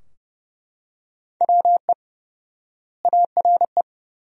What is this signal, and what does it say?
Morse code, heard as a keyed tone, about 30 wpm, 700 Hz. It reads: WE ARE
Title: 4x Word-Spacing - Sentences from Top 400 Words - 30wpm